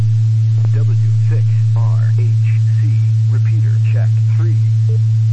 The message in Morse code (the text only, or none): none